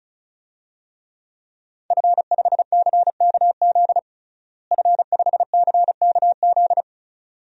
Morse code, heard as a keyed tone, about 34 wpm, 700 Hz. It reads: F5CKZ F5CKZ
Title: DXCC-100 Call Signs - 5 Characters - Repeated 2x - 35wpm